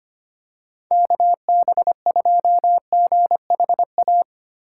Morse code, heard as a keyed tone, about 25 wpm, 700 Hz. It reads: KB2GHA
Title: USA Call Signs - 6 Characters - 3x Character Spacing - 25wpm